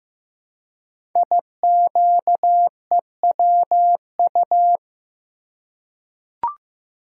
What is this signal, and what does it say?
Morse code, heard as a keyed tone, about 15 wpm, 700 Hz. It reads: IQEWU
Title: Code Groups - 5 Letters - 15wpm